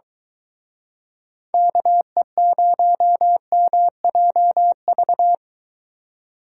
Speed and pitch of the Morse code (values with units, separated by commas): 23 wpm, 700 Hz